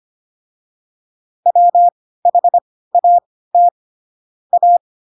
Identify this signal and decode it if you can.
Morse code, heard as a keyed tone, about 25 wpm, spaced as for 16 wpm, 700 Hz. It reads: WHAT A
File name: Pinocchio / Pinocchio ch1 10 25 ik1hge